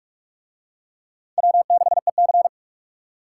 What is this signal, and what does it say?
Morse code, heard as a keyed tone, about 44 wpm, 700 Hz. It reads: W6EC